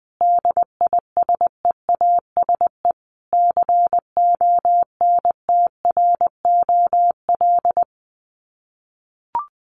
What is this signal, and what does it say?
Morse code, heard as a keyed tone, about 20 wpm, 700 Hz. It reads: DISEASE CONTROL
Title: Sets of 5 Words Encore2 20wpm